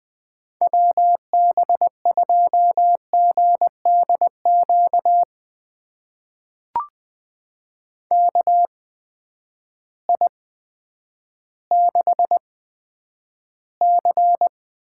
Morse code, heard as a keyed tone, about 21 wpm, spaced as for 10 wpm, 700 Hz. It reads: WB2GDQ KI6C
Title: USA Call Signs - 6 Characters - 8x Character Spacing - 20wpm